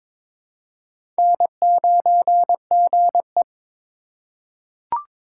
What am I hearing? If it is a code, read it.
Morse code, heard as a keyed tone, about 22 wpm, 690 Hz. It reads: N9GE
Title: USA Call Signs - 4 Characters - 3x Character Spacing - 22wpm